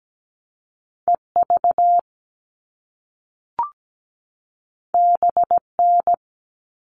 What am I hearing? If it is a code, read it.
Morse code, heard as a keyed tone, about 17 wpm, 700 Hz. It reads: EV BN
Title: All Cut Numbers - 2-Digit 17wpm